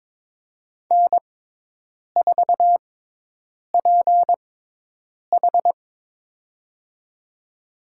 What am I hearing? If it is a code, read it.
Morse code, heard as a keyed tone, about 22 wpm, spaced as for 8 wpm, 700 Hz. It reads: N4PH
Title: USA Call Signs - 6x Character Spacing - 22wpm